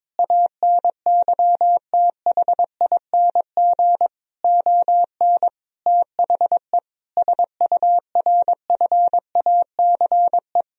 Morse code, heard as a keyed tone, about 22 wpm, 700 Hz. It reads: ANYTHING ON THE SURFACE
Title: Sentences from Top 1000 Words - Review Part 3 - 22wpm